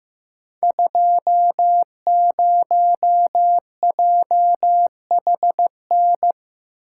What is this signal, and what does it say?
Morse code, heard as a keyed tone, about 15 wpm, 700 Hz. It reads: 20JHN
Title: Code Groups - 5 Letter:Numbers - 15wpm